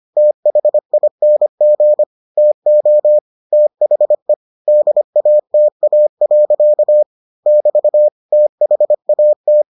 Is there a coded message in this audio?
Yes: Morse code from 0.2 to 9.6 s, heard as a keyed tone, about 25 wpm, 600 Hz. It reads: THING TO THE DATA. <BT> THAT